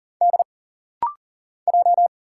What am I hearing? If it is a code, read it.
Morse code, heard as a keyed tone, about 40 wpm, 700 Hz. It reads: D J
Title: Single Letters - Rapid-Fire 40wpm